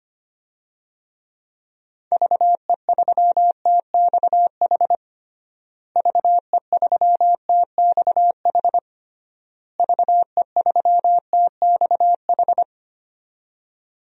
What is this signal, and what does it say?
Morse code, heard as a keyed tone, about 25 wpm, 700 Hz. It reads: VE3TXH VE3TXH VE3TXH
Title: DXCC-100 Call Signs - 6 Characters - Repeated 3x - 25wpm